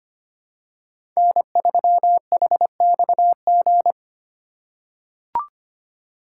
Morse code, heard as a keyed tone, about 25 wpm, 700 Hz. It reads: N3HXG
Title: USA Call Signs - 5 Characters - 4x Character Spacing - 25wpm